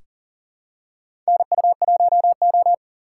Morse code, heard as a keyed tone, about 37 wpm, 700 Hz. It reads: NU1O
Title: K1USN SST QSO Practice - 40wpm